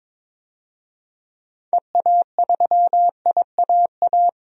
Morse code, heard as a keyed tone, about 22 wpm, 700 Hz. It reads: EA3IAA
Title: DXCC-100 Call Signs - 6 Characters - 22wpm